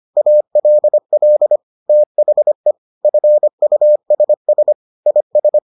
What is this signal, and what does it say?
Morse code, heard as a keyed tone, about 25 wpm, 600 Hz. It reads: ALL THE FUSS IS